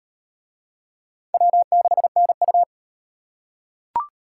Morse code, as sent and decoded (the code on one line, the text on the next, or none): .-- -.... -. ..-
W6NU